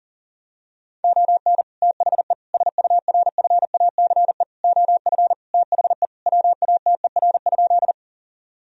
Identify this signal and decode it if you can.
Morse code, heard as a keyed tone, about 40 wpm, 700 Hz. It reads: ON THE SURFACE OF THE WATER?
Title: Sentences from Top 1000 Words - Review Part 3 - 40wpm